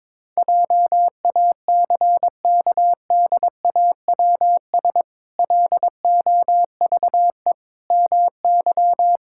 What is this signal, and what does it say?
Morse code, heard as a keyed tone, about 22 wpm, 700 Hz. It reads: JACKDAWS LOVE MY
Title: Pangram Challenge v2 - 22wpm